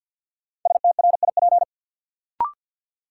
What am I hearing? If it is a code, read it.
Morse code, heard as a keyed tone, about 53 wpm, 700 Hz. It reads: STRIP
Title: Sentences from Top 1600 Words - New Word Drill Alt - 50wpm